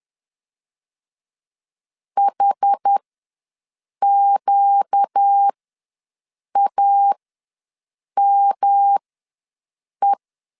Morse code, heard as a keyed tone, about 10 wpm, spaced as for 6 wpm, 800 Hz. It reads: HQAME
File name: LEZIONI / 5 LBVC / 5 LBV ws 10wpm 3645